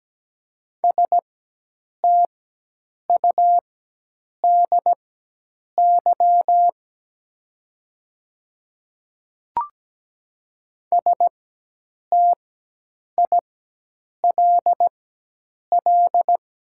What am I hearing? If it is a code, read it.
Morse code, heard as a keyed tone, about 17 wpm, spaced as for 8 wpm, 700 Hz. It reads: STUDY STILL
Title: 4x Character Spacing - Top 300 Words - 17wpm